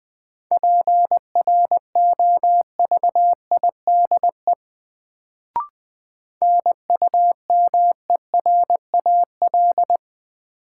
Morse code, heard as a keyed tone, about 20 wpm, 700 Hz. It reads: PROVIDE NUMERAL
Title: Sentences from Top 1000 Words - Review Part 2 - New Word Drill - 20wpm